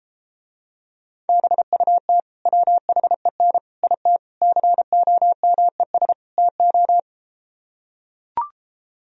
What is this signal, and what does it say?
Morse code, heard as a keyed tone, about 33 wpm, 700 Hz. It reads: BUT WHEN IT COMES TO